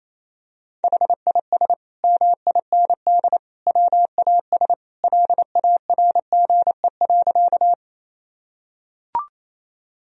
Morse code, heard as a keyed tone, about 28 wpm, 700 Hz. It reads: HIS MIND WAS LARGE.
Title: Sentences from Top 1000 Words - Review Part 1 - 28wpm